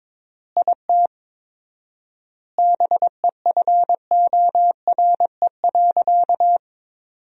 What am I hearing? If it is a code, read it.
Morse code, heard as a keyed tone, about 22 wpm, 700 Hz. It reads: IT BEFORE.
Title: 4x Word-Spacing - Sentences from Top 400 Words - 22wpm